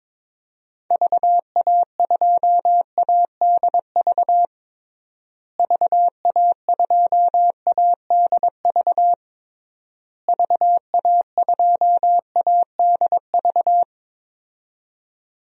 Morse code, heard as a keyed tone, about 22 wpm, 700 Hz. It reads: VA2ADV VA2ADV VA2ADV
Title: DXCC-100 Call Signs - 6 Characters - Repeated 3x - 22wpm